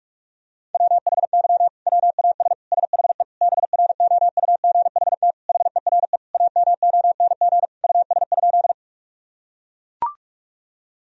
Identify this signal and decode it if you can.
Morse code, heard as a keyed tone, about 41 wpm, 700 Hz. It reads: WHY WAS SHE BROUGHT HERE AMONG US?
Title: Sentences from Top 1000 Words - Review Part 3 - 45wpm